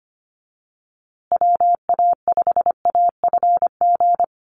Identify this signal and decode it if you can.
Morse code, heard as a keyed tone, about 25 wpm, 700 Hz. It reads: WA5AFG
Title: USA Call Signs Repeated 2x 25wpm